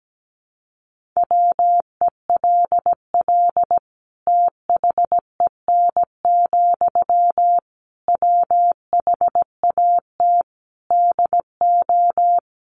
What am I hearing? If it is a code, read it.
Morse code, heard as a keyed tone, about 17 wpm, 700 Hz. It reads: WELL THEN, WHAT DO
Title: Sentences from Top 200 Words 17wpm